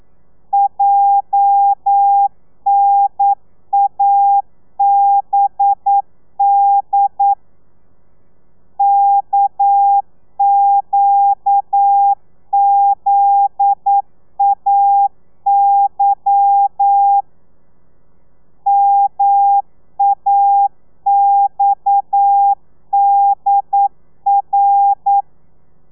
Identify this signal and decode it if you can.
Morse code, heard as a keyed tone, about 9 wpm, 800 Hz. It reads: JNABD KQZAY MAXDR
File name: telegrafia / i2tfj / Lez11 40